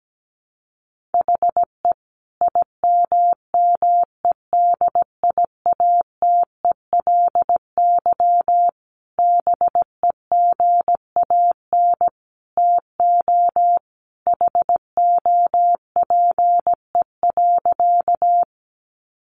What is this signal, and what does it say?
Morse code, heard as a keyed tone, about 17 wpm, 700 Hz. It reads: HE IMMEDIATELY BEGAN TO HOPE.
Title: Sentences from Top 1100 Words 17wpm